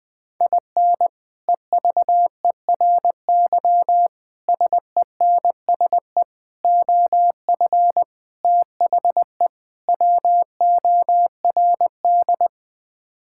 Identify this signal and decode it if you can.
Morse code, heard as a keyed tone, about 21 wpm, 700 Hz. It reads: IN EVERY SENSE OF THE WORD
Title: Sets of 6 Words - 20wpm